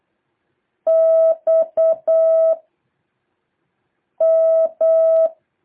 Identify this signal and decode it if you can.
Morse code, heard as a keyed tone, about 8 wpm, 650 Hz. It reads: X M
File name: jh8cbh / 2023cw